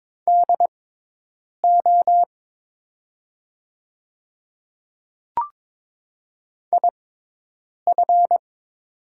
Morse code, heard as a keyed tone, about 22 wpm, spaced as for 8 wpm, 700 Hz. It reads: DO IF
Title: 6x Character Spacing - Top 100 Words - 22wpm